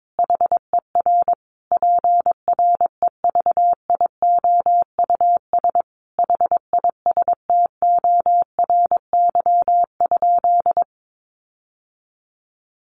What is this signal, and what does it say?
Morse code, heard as a keyed tone, about 22 wpm, 700 Hz. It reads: HER PREVIOUS HISTORY?
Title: Sentences from Top 1500 Words 22wpm